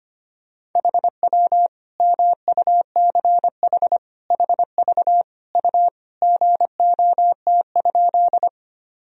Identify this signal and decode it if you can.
Morse code, heard as a keyed tone, about 25 wpm, 700 Hz. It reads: HW MUCH HV U GOT?
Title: Sentences from Top 300 Words and Abbreviations 25wpm